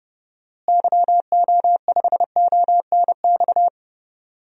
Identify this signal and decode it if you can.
Morse code, heard as a keyed tone, about 30 wpm, 700 Hz. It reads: YO5ONX
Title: DXCC-100 Call Signs - 6 Characters - Repeated 2x - 30wpm